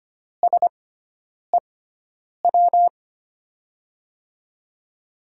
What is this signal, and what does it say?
Morse code, heard as a keyed tone, about 25 wpm, spaced as for 9 wpm, 700 Hz. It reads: SEW
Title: USA Call Signs - 5 Characters - 6x Character Spacing - 25wpm